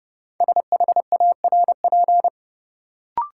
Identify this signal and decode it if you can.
Morse code, heard as a keyed tone, about 31 wpm, 700 Hz. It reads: SHARP
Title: Sentences from Top 1600 Words - New Word Drill - 30wpm